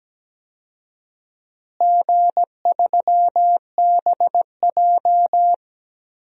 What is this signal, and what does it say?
Morse code, heard as a keyed tone, about 17 wpm, 700 Hz. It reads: G3BJ
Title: DXCC-100 Call Signs - 4 Characters - Repeated 2x - 17wpm